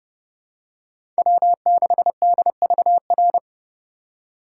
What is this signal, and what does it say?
Morse code, heard as a keyed tone, about 30 wpm, 700 Hz. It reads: W6DVR
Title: USA Call Signs - 4x Character Spacing - 30wpm